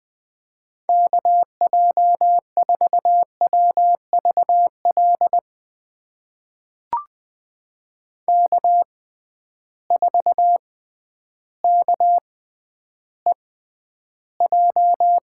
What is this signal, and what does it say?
Morse code, heard as a keyed tone, about 20 wpm, spaced as for 11 wpm, 700 Hz. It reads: KJ4WVL K4KEJ
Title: USA Call Signs - 6x Character Spacing - 20wpm